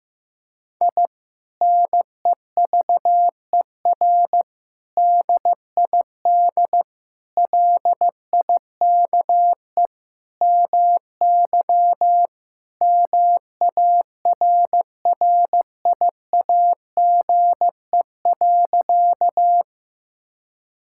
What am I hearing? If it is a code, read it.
Morse code, heard as a keyed tone, about 15 wpm, 700 Hz. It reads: I NEVER DID LIKE MY MARRIAGE.
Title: Sentences from Top 1000 Words - Review Part 5 - 15wpm